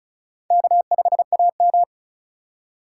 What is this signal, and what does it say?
Morse code, heard as a keyed tone, about 36 wpm, 700 Hz. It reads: K5AM
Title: K1USN SST QSO Practice - 35wpm